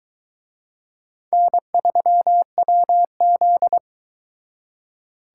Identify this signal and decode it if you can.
Morse code, heard as a keyed tone, about 22 wpm, 700 Hz. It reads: N3WZ